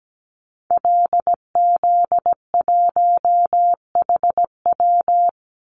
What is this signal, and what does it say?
Morse code, heard as a keyed tone, about 17 wpm, 700 Hz. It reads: LZ1HW
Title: Call Signs Repeated 3x - 17wpm